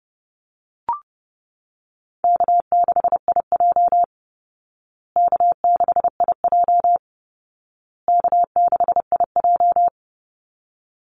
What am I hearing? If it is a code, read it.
Morse code, heard as a keyed tone, about 30 wpm, 700 Hz. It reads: K6IJ K6IJ K6IJ
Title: Easy Call Signs Repeated 3x - 30wpm